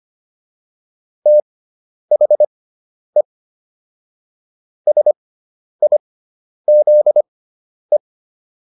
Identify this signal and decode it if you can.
Morse code, heard as a keyed tone, about 25 wpm, spaced as for 10 wpm, 600 Hz. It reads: THE SIZE